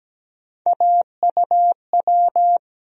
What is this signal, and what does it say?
Morse code, heard as a keyed tone, about 17 wpm, 700 Hz. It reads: AUW
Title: USA Call Sign Postfixes - 3 Letters 17wpm